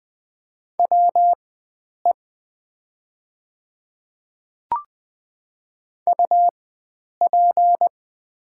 Morse code, heard as a keyed tone, about 20 wpm, spaced as for 9 wpm, 700 Hz. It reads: WE UP
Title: 4x Character Spacing - Top 300 Words - 20wpm